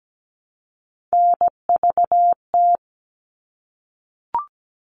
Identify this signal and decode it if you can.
Morse code, heard as a keyed tone, about 17 wpm, 700 Hz. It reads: NVT
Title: All Cut Numbers - 3-Digit 17wpm